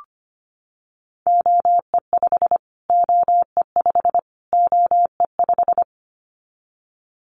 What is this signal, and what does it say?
Morse code, heard as a keyed tone, about 25 wpm, 700 Hz. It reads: OE5 OE5 OE5
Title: Callsign Prefixes for Top 25 DXCC - Repeated 3x - 25wpm